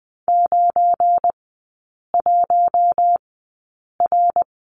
Morse code, heard as a keyed tone, about 20 wpm, spaced as for 8 wpm, 700 Hz. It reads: 91R
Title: 172 Lesson 28e - Number 9 - Three Characters 20wpm